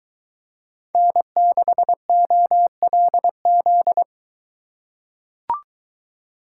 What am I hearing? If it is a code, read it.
Morse code, heard as a keyed tone, about 22 wpm, 700 Hz. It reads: N6OLZ